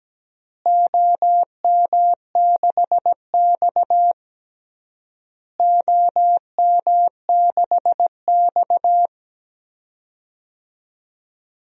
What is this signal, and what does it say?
Morse code, heard as a keyed tone, about 17 wpm, 700 Hz. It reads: OM6X OM6X
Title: DXCC-100 Call Signs - 4 Characters - Repeated 3x - 17wpm